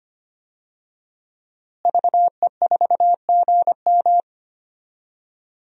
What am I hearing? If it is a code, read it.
Morse code, heard as a keyed tone, about 25 wpm, 700 Hz. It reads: VE4GM
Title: DXCC-100 Call Signs - 5 Characters - Repeated 3x - 25wpm